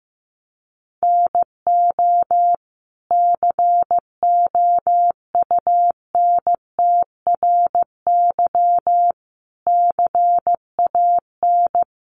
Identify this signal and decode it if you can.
Morse code, heard as a keyed tone, about 15 wpm, 700 Hz. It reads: NO COUNTRY CAN
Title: Sentences from Top 800 Words 15wpm